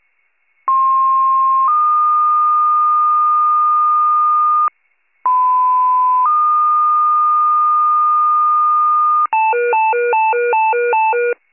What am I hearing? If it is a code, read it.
Two-tone page, heard as a one-second tone, then a three-second tone, about 1060 then 1220 Hz, about 1010 then 1220 Hz. It is heard from 0.7 to 9.3 s.